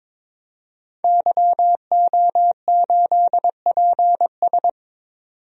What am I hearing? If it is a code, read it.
Morse code, heard as a keyed tone, about 22 wpm, 700 Hz. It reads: YO8PS
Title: DXCC-100 Call Signs - 5 Characters - 22wpm